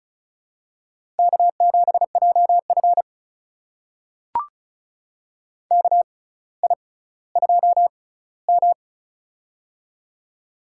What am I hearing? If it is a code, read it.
Morse code, heard as a keyed tone, about 36 wpm, spaced as for 18 wpm, 700 Hz. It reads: K7JF KI2M
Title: USA Call Signs - 4 Characters - 6x Character Spacing - 35wpm